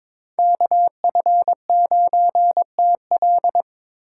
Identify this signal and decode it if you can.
Morse code, heard as a keyed tone, about 26 wpm, 700 Hz. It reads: KF9TL
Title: Speed Racing - USA 2x2 Call Signs 22wpm